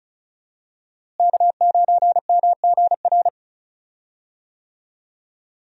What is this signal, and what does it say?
Morse code, heard as a keyed tone, about 36 wpm, 700 Hz. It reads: K9MGR
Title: USA Call Signs - 5 Characters - Repeated 3x - 35wpm